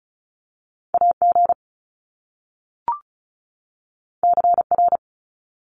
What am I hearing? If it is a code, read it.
Morse code, heard as a keyed tone, about 36 wpm, 700 Hz. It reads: AG CR
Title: Top Two Letter Combinations in Words 35wpm